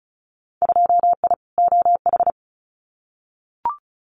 Morse code, heard as a keyed tone, about 36 wpm, 700 Hz. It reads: 2I OH
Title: Winter Field Day Exchanges 35wpm